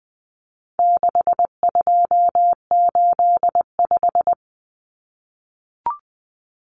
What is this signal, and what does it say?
Morse code, heard as a keyed tone, about 20 wpm, 700 Hz. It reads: 6285
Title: Common Cut Numbers - 4-Digit 20wpm